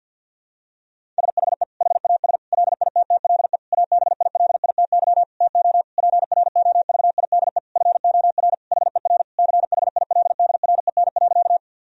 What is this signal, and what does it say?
Morse code, heard as a keyed tone, about 51 wpm, 700 Hz. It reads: SHE HAS LITTLE ABILITY TO PROVIDE FOR HER CHILDREN.
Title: Sentences from Top 1000 Words - Review Part 3 - 50wpm